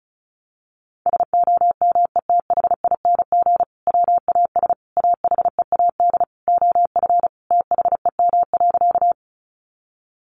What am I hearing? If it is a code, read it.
Morse code, heard as a keyed tone, about 34 wpm, 700 Hz. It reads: SOMETHING WAS AHEAD OF THEM.
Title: Sentences from Top 1100 Words 35wpm